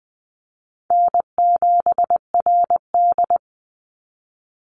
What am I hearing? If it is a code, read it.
Morse code, heard as a keyed tone, about 20 wpm, 700 Hz. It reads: N7RD
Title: Call Signs - 20wpm